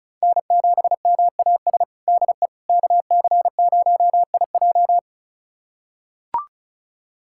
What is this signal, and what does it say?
Morse code, heard as a keyed tone, about 36 wpm, 700 Hz. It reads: N7MAS DE KC0IJ
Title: USA Callsign DE Callsign - 5 Characters - 35wpm